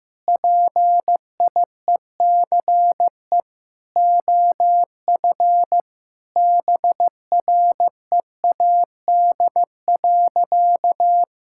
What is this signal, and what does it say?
Morse code, heard as a keyed tone, about 15 wpm, 700 Hz. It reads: PIECE OF BREAD.
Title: Sentences from Top 1600 Words 15wpm